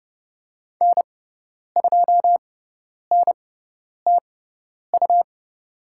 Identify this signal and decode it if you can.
Morse code, heard as a keyed tone, about 30 wpm, spaced as for 10 wpm, 700 Hz. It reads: N2NTU
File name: Lesson 18g - Letter m - Call Signs 30wpm